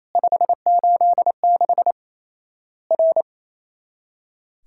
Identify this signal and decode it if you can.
Morse code, heard as a keyed tone, about 28 wpm, 700 Hz. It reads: 586 R
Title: Worked All Europe DX Contest - QTC Exchange - 28wpm